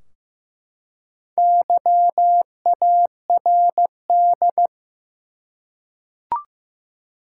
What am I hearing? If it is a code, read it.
Morse code, heard as a keyed tone, about 15 wpm, 700 Hz. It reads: YARD
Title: Four Letter Words - 15wpm